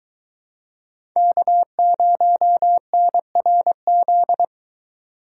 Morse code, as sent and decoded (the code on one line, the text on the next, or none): -.- ----- -. .-. --..
K0NRZ